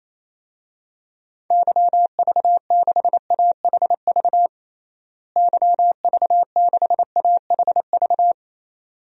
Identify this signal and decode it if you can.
Morse code, heard as a keyed tone, about 28 wpm, 700 Hz. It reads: YV6AHV YV6AHV
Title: DXCC-100 Call Signs - 6 Characters - Repeated 2x - 28wpm